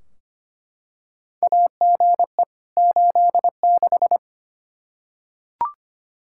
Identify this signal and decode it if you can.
Morse code, heard as a keyed tone, about 25 wpm, 700 Hz. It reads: AGE 86
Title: K1USN SST QSO Practice 25wpm